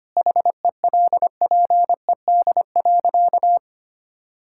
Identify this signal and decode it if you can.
Morse code, heard as a keyed tone, about 25 wpm, 700 Hz. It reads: HELPED.
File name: practice / 25wpm / Sentences with 4-Word Phrases - 25wpm